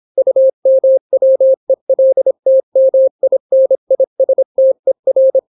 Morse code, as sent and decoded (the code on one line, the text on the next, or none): ..- -- .-- . .-.. - -- .. -. .. ... - . .-.
UMWELTMINISTER